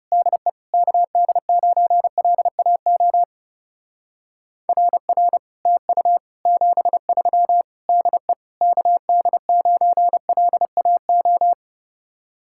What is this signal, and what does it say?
Morse code, heard as a keyed tone, about 37 wpm, 700 Hz. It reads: DE KD9LAO RR TU 73 DE KD9LAO
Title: Speed-Racing Common QSO Phrases - 30wpm